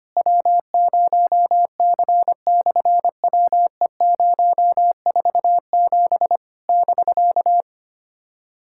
Morse code, heard as a keyed tone, about 25 wpm, 700 Hz. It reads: W0C/WE047 <BK>
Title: SOTA QSO Practice 25wpm